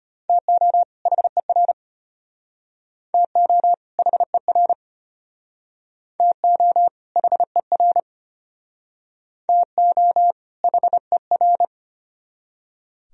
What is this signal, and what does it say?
Morse code, heard as a keyed tone, about 31 wpm, 700 Hz. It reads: TO HER TO HER TO HER TO HER
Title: Speed-Racing - Sets of 2 Words 25wpm